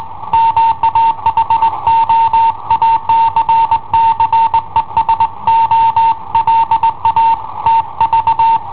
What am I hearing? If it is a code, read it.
Morse code, heard as a keyed tone, about 21 wpm, 910 Hz. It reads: MA HO ACCESO LA TV
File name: i7ohp / cassetta i7ohp B